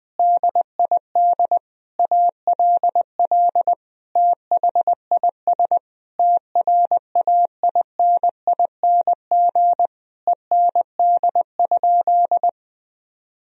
Morse code, heard as a keyed tone, about 20 wpm, 700 Hz. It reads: DID ALL THIS TRAINING END?
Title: Sentences from Top 1000 Words - Review Part 4 - 20wpm